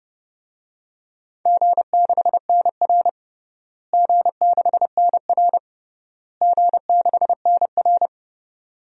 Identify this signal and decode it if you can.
Morse code, heard as a keyed tone, about 31 wpm, 700 Hz. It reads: G6NR G6NR G6NR
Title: DXCC-100 Call Signs - 4 Characters - Repeated 3x - 30wpm